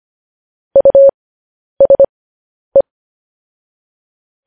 Morse code, heard as a keyed tone, about 25 wpm, spaced as for 10 wpm, 550 Hz. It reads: USE
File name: podcast / News headlines-1x-25x10